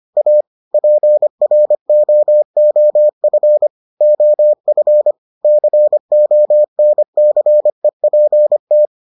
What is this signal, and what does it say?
Morse code, heard as a keyed tone, about 25 wpm, 600 Hz. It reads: A PROOF OF CONCEPT